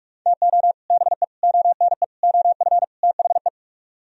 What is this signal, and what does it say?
Morse code, heard as a keyed tone, about 47 wpm, 700 Hz. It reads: TO BE ONE OF THE